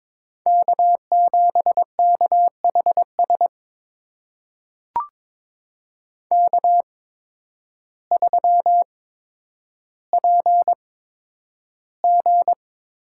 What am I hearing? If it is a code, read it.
Morse code, heard as a keyed tone, about 22 wpm, spaced as for 10 wpm, 700 Hz. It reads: K7KHS K3PG
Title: USA Call Signs - 5 Characters - 8x Character Spacing - 22wpm